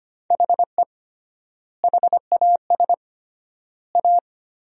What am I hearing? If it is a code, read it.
Morse code, heard as a keyed tone, about 25 wpm, 700 Hz. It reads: HE HAS A
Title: 3x Word-Spacing - Sentences from Top 500 Words - 25wpm